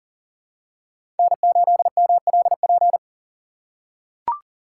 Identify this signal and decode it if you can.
Morse code, heard as a keyed tone, about 40 wpm, 700 Hz. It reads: N8MLP
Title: USA Call Signs - 5 Characters - 4x Character Spacing - 40wpm